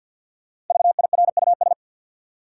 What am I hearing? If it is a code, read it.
Morse code, heard as a keyed tone, about 50 wpm, 700 Hz. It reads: VIRUS
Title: Five Letter Words - 50wpm